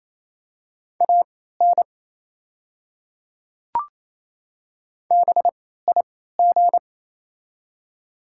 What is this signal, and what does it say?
Morse code, heard as a keyed tone, about 28 wpm, spaced as for 16 wpm, 700 Hz. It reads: AN BIG
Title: 3x Character Spacing - Top 200 Words - 28wpm